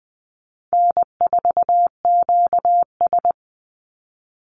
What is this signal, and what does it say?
Morse code, heard as a keyed tone, about 19 wpm, 700 Hz. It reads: N4QS
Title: Easy Call Signs Repeated 2x - 20wpm